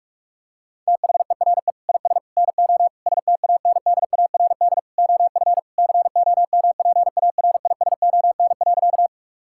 Morse code, heard as a keyed tone, about 42 wpm, 700 Hz. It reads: THERE IS NO STANDARD OF COMPARISON.
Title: Sentences from Top 1600 Words 45wpm